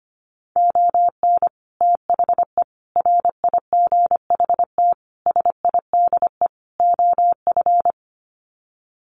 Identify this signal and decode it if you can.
Morse code, heard as a keyed tone, about 25 wpm, 700 Hz. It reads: ON THE RIGHT SIDE OF
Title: Sets of 5 Words Encore 25wpm